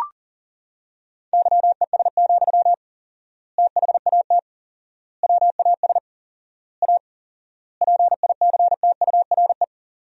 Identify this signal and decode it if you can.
Morse code, heard as a keyed tone, about 37 wpm, 700 Hz. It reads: YES, THAT WAS A PICTURE
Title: 4x Word-Spacing - Sentences from Top 500 Words - 40wpm